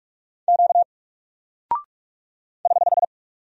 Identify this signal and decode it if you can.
Morse code, heard as a keyed tone, about 44 wpm, 700 Hz. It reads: Q <HH>
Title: Single Character (Letter:Number:Punctuation:Prosign) 45wpm